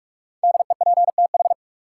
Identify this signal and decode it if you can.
Morse code, heard as a keyed tone, about 46 wpm, 700 Hz. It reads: DEPTH
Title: Sentences from Top 1200 Words - New Word Drill Alt - 45wpm